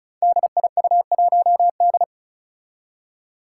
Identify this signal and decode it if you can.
Morse code, heard as a keyed tone, about 36 wpm, 700 Hz. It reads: DIU1D
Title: Five Challenging Letter-Number Combinations 35wpm